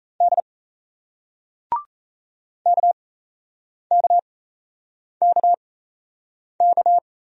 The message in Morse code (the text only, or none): D KKKK